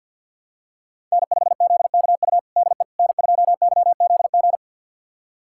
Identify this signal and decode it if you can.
Morse code, heard as a keyed tone, about 46 wpm, 700 Hz. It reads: N5ZKU DE N2YZG
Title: USA Callsign DE Callsign - 5 Characters - 50wpm